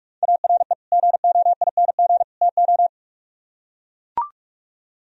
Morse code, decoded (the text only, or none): ARE GOING TO